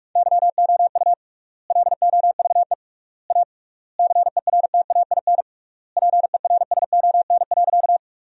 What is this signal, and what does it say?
Morse code, heard as a keyed tone, about 47 wpm, 700 Hz. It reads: YOU LOVE A CERTAIN PERSON.
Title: 3x Word-Spacing - Sentences from Top 500 Words - 45wpm